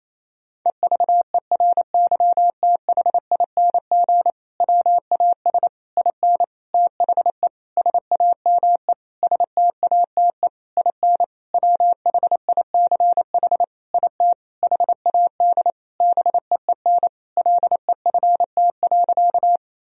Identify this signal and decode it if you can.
Morse code, heard as a keyed tone, about 27 wpm, 700 Hz. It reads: EVERYTHING WAS IN THE SAME STATE IN WHICH IT HAD BEEN LEFT.
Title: Sentences from Top 1000 Words - Review Part 2 - 28wpm